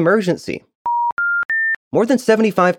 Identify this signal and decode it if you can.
Special information tone, with speech.